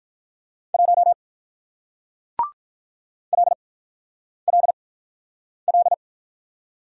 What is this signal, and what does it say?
Morse code, heard as a keyed tone, about 48 wpm, spaced as for 9 wpm, 700 Hz. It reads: 1 LLL